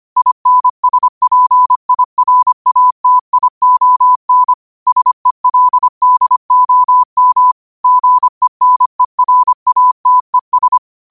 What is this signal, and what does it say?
Morse code, heard as a keyed tone, about 25 wpm, 1000 Hz. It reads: INSPIRATION SELDOM GENERATES
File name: qotdfiles / 25wpm